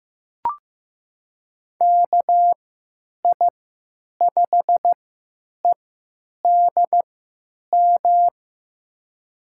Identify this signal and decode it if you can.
Morse code, heard as a keyed tone, about 15 wpm, spaced as for 9 wpm, 700 Hz. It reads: KI5EDM
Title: USA Call Signs - 6 Characters - 3x Character Spacing - 15wpm